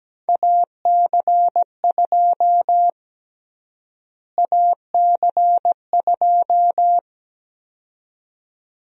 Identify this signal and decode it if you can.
Morse code, heard as a keyed tone, about 17 wpm, 700 Hz. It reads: AC2 AC2
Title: USA Call Sign Prefixes - Repeated 2x - 17wpm